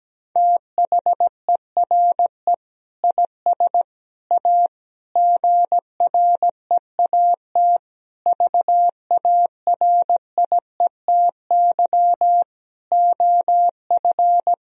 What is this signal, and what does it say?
Morse code, heard as a keyed tone, about 17 wpm, 700 Hz. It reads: THERE IS A GREAT VARIETY OF
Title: Sentences from Top 1000 Words - Review Part 4 - 17wpm